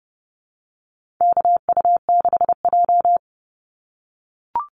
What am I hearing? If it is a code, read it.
Morse code, heard as a keyed tone, about 30 wpm, 700 Hz. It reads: KU6J
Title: NA SOTA Activator Callsigns 30wpm